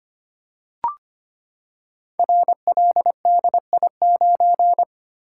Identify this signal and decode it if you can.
Morse code, heard as a keyed tone, about 25 wpm, 690 Hz. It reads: RLDI9
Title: Code Groups - 5 Letter:Numbers - 25wpm